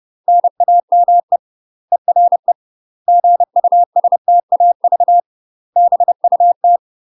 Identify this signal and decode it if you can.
Morse code, heard as a keyed tone, about 30 wpm, 700 Hz. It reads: NAME ERE GUSTAV BUT
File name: UK MASTER QSOs TRAINING / 30WPM / UK MASTER QSOs 0023